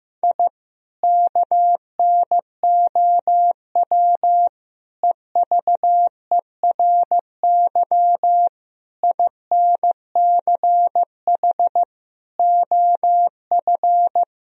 Morse code, heard as a keyed tone, about 15 wpm, 700 Hz. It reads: I KNOW EVERY INCH OF
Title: Sentences from Top 1000 Words - Review Part 3 - 15wpm